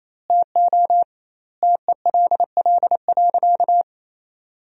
Morse code, heard as a keyed tone, about 27 wpm, 700 Hz. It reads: TO TELL.
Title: 2x Word-Spacing - Sentences from Top 200 Words - 28wpm